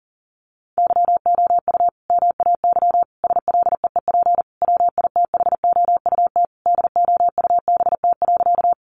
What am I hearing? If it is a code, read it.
Morse code, heard as a keyed tone, about 40 wpm, 700 Hz. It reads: YOU MAY SLEEP WITHOUT DOUBT.
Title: Sentences from Top 1500 Words 40wpm